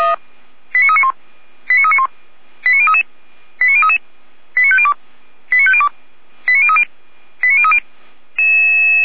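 Five-tone selective call (ZVEI), heard as quick runs of five tones, tones of about 70 ms. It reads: 89391 89391 89033 89033 89502 89502 89030 89030
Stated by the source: ZVEI five-tone selective call received on SDR.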